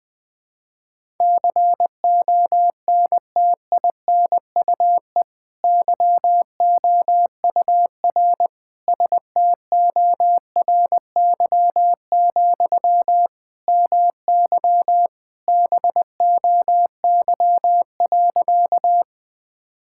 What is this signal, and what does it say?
Morse code, heard as a keyed tone, about 21 wpm, 700 Hz. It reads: CONTINUE YOUR STORY, MY BOY.
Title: Sentences from Top 1000 Words - Review Part 6 - 20wpm